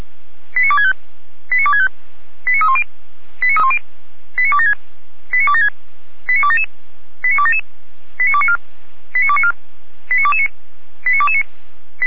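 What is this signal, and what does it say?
Five-tone selective call (ZVEI), heard as quick runs of five tones, tones of about 70 ms. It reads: 89265 89265 89310 89276 89276 89277 89277 89294 89294 89229 89229
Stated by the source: Five-tone ZVEI selective call received by SDR.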